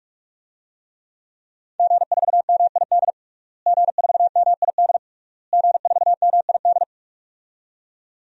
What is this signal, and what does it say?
Morse code, heard as a keyed tone, about 41 wpm, 700 Hz. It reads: G4MID G4MID G4MID
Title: DXCC-100 Call Signs - 5 Characters - Repeated 3x - 45wpm